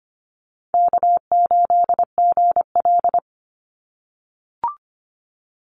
Morse code, heard as a keyed tone, about 25 wpm, 700 Hz. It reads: K8GL
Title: Call Signs - 25wpm